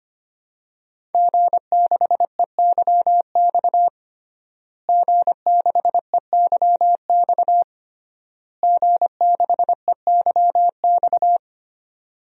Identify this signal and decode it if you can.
Morse code, heard as a keyed tone, about 25 wpm, 700 Hz. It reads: G6EYX G6EYX G6EYX
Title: DXCC-100 Call Signs - 5 Characters - Repeated 3x - 25wpm